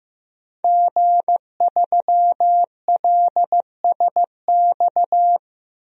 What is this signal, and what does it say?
Morse code, heard as a keyed tone, about 15 wpm, 700 Hz. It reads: G3LSX
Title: DXCC-100 Call Signs - 5 Characters - 15wpm